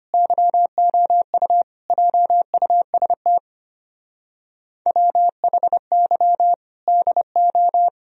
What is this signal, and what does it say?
Morse code, heard as a keyed tone, about 31 wpm, 700 Hz. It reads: YOU JUST WHY DO
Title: Speed-Racing - Sets of 5 Words 25wpm